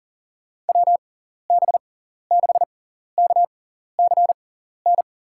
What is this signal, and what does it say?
Morse code, heard as a keyed tone, about 40 wpm, spaced as for 14 wpm, 700 Hz. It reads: WB6KCN
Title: USA Call Signs - 6x Character Spacing - 40wpm